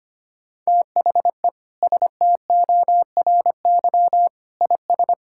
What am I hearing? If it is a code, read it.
Morse code, heard as a keyed tone, about 25 wpm, 700 Hz. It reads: THE STORY IS
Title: Sentences from Top 1000 Words - Review Part 5 - 25wpm